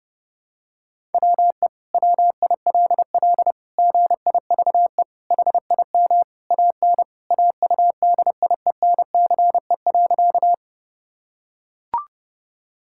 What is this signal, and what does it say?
Morse code, heard as a keyed tone, about 32 wpm, 700 Hz. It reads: WE WILL GIVE HIM AN AUDIENCE.
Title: Sentences from Top 1000 Words - Review Part 5 - 30wpm